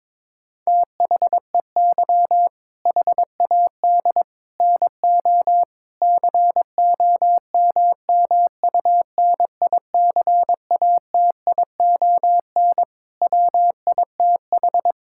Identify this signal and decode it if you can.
Morse code, heard as a keyed tone, about 22 wpm, 700 Hz. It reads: THEY HAD NO COMMUNICATION WITH
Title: Sentences from Top 1000 Words - Review Part 6 - 22wpm